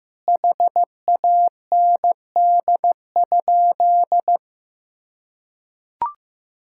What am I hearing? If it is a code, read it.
Morse code, heard as a keyed tone, about 15 wpm, 700 Hz. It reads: HAND?
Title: Sentences from Top 1000 Words - Review Part 5 - 15wpm